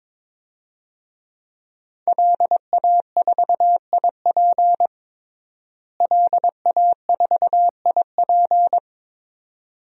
Morse code, heard as a keyed tone, about 23 wpm, 700 Hz. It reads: LA4IP LA4IP
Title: DXCC-100 Call Signs - 5 Characters - Repeated 2x - 22wpm